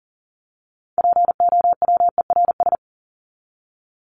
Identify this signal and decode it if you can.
Morse code, heard as a keyed tone, about 40 wpm, 700 Hz. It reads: POWERS
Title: ES:S Suffix - Top 100 Words with Base Words 40wpm